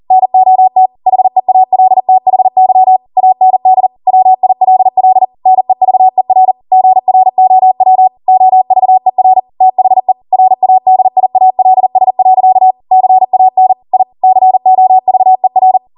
Morse code, heard as a keyed tone, about 40 wpm, 750 Hz. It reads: NOT HEALTHY AND WILL NEVER GROW OVER THE RADIALS. CAN I COVER
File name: Morse / 40WPM